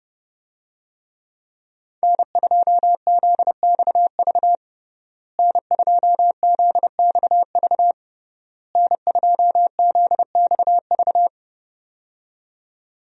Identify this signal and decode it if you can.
Morse code, heard as a keyed tone, about 32 wpm, 700 Hz. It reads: N2ZXV N2ZXV N2ZXV
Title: USA Call Signs - 5 Characters - Repeated 3x - 30wpm